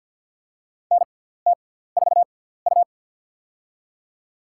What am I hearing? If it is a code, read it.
Morse code, heard as a keyed tone, about 53 wpm, spaced as for 17 wpm, 700 Hz. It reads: NT4U
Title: USA Call Signs - 4 Characters - 6x Character Spacing - 50wpm